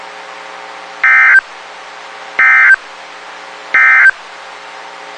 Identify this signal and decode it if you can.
Emergency Alert System (SAME) tones, heard as harsh buzzing bursts. It is heard from 0.9 to 4.2 s.
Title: WXK91 NOAA Weather Radio